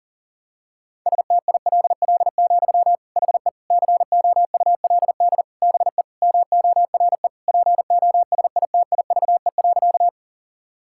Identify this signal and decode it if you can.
Morse code, heard as a keyed tone, about 40 wpm, 700 Hz. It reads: STILL, HE COULD BE MORE POSITIVE.
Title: Sentences from Top 1000 Words - Review Part 5 - 40wpm